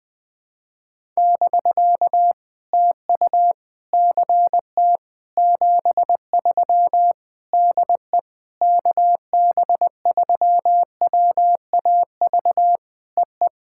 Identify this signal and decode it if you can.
Morse code, heard as a keyed tone, about 20 wpm, 700 Hz. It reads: <BK> TU CT 73 DE KB3WAV EE
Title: POTA QSO Practice - 20wpm